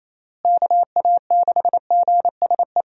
Morse code, heard as a keyed tone, about 28 wpm, 700 Hz. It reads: KA6GSE
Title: USA Call Signs - 6 Characters - Repeated 3x - 28wpm